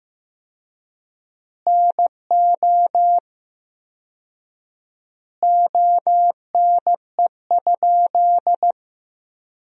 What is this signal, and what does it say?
Morse code, heard as a keyed tone, about 15 wpm, 700 Hz. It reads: NO ONE?
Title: 4x Word-Spacing - Sentences from Top 300 Words - 15wpm